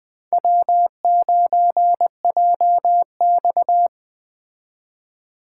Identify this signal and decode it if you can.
Morse code, heard as a keyed tone, about 20 wpm, 700 Hz. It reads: W9JX
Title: Speed Racing - USA 1x2 Call Signs 17wpm